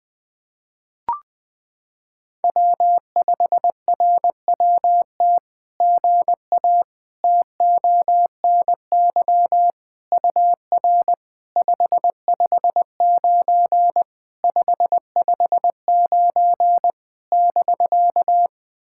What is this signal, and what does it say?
Morse code, heard as a keyed tone, about 21 wpm, 700 Hz. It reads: W5RWT GA TONY UR 559 559 <BK>
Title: SOTA QSO Practice 20wpm